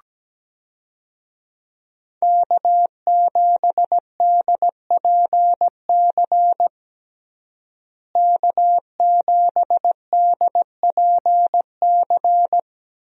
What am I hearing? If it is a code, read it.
Morse code, heard as a keyed tone, about 17 wpm, 700 Hz. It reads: K7DPC K7DPC
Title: USA Call Signs - 5 Characters - Repeated 2x - 17wpm